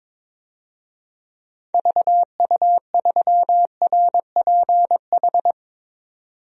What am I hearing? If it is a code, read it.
Morse code, heard as a keyed tone, about 21 wpm, 700 Hz. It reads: VU3RPH
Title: DXCC-100 Call Signs - 6 Characters - 22wpm